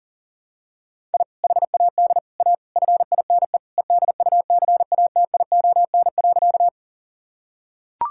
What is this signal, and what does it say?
Morse code, heard as a keyed tone, about 40 wpm, 700 Hz. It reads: I HAD A FINE EDUCATION.
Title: Sentences from Top 1000 Words - Review Part 6 - 40wpm